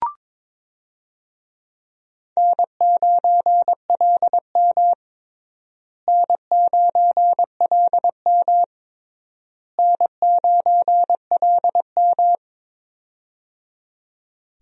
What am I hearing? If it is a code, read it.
Morse code, heard as a keyed tone, about 22 wpm, 700 Hz. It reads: N9LM N9LM N9LM
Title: USA Call Signs - 4 Characters - Repeated 3x - 22wpm